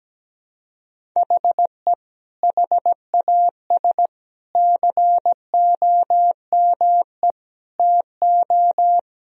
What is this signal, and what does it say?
Morse code, heard as a keyed tone, about 17 wpm, 700 Hz. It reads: HE HAS COME TO
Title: Sentences from Top 1000 Words - Review Part 1 - 17wpm